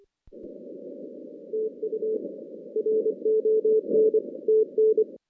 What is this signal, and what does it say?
Morse code, heard as a keyed tone, about 26 wpm, 410 Hz. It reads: TU RON TN
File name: 2016 JAN CW NAQP / 2016 Jan naqp k5ot - 0-8